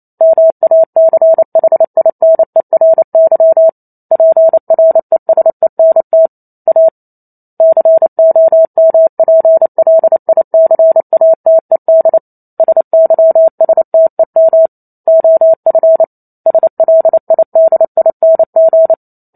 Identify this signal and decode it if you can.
Morse code, heard as a keyed tone, about 28 wpm, 650 Hz. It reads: MACHINERY PRESENT A COMPLICATED SYSTEM OF SLIDING
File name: Chapter 2 - 28 WPM